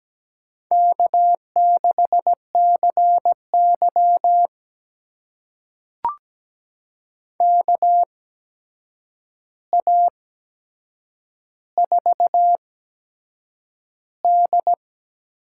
Morse code, heard as a keyed tone, about 17 wpm, spaced as for 7 wpm, 700 Hz. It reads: K6CY KA4D
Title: USA Call Signs - 4 Characters - 8x Character Spacing - 17wpm